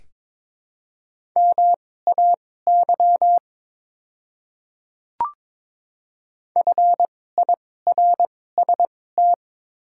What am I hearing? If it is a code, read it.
Morse code, heard as a keyed tone, about 21 wpm, spaced as for 16 wpm, 700 Hz. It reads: MAY FIRST
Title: 2x Character Spacing - Top 200 Words - 22wpm